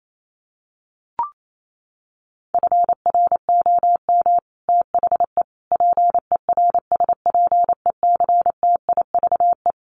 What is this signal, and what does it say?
Morse code, heard as a keyed tone, about 28 wpm, 700 Hz. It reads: FROM THE PERSPECTIVE
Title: Sets of 5 Words Encore2 28wpm